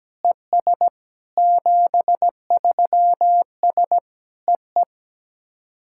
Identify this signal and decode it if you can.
Morse code, heard as a keyed tone, about 17 wpm, 700 Hz. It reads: ES 73S EE
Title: SOTA QSO Practice 17wpm